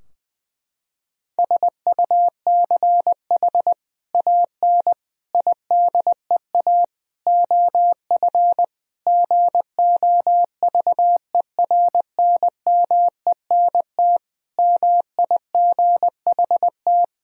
Morse code, heard as a keyed tone, about 20 wpm, 700 Hz. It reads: SUCH AN IDEA OF GOVERNMENT MIGHT
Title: Sentences from Top 1000 Words - Review Part 3 - 20wpm